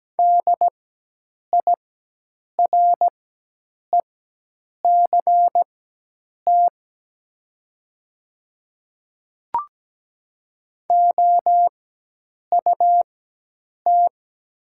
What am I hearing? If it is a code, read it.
Morse code, heard as a keyed tone, about 17 wpm, spaced as for 8 wpm, 700 Hz. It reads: DIRECT OUT
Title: 4x Character Spacing - Top 500 Words - 17wpm